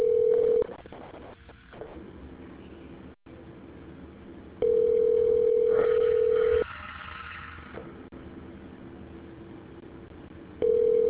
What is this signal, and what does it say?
Ringback tone (North America), 440 + 480 Hz.